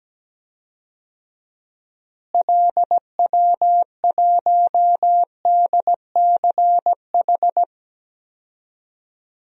Morse code, heard as a keyed tone, about 17 wpm, 700 Hz. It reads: LW1DCH
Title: DXCC-100 Call Signs - 6 Characters - Repeated 2x - 17wpm